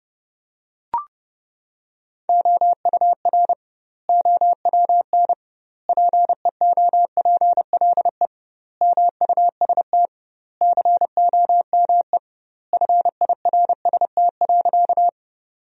Morse code, heard as a keyed tone, about 30 wpm, 700 Hz. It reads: OUR OWN PEOPLE MUST COME FIRST.
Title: 2x Word-Spacing - Sentences from Top 300 Words - 30wpm